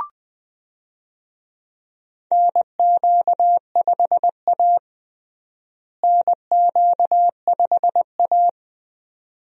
Morse code, heard as a keyed tone, about 20 wpm, 700 Hz. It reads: NQ5A NQ5A
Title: USA Call Signs - 4 Characters - Repeated 2x - 20wpm